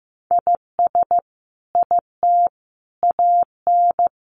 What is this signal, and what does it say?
Morse code, heard as a keyed tone, about 15 wpm, 700 Hz. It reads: IS IT AN
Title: Sentences from Top 1500 Words 15wpm